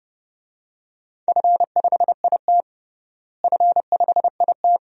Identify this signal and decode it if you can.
Morse code, heard as a keyed tone, about 30 wpm, 700 Hz. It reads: F5IT F5IT
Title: DXCC-100 Call Signs - 4 Characters - Repeated 2x - 30wpm